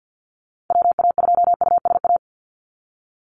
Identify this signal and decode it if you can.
Morse code, heard as a keyed tone, about 50 wpm, 700 Hz. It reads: WA2USA
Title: NA SOTA Activator Callsigns 50wpm